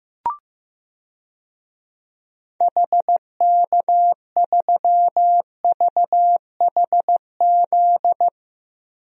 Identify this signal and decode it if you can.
Morse code, heard as a keyed tone, about 15 wpm, 700 Hz. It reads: HK3VHZ
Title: DXCC-100 Call Signs - 6 Characters - Repeated 2x - 15wpm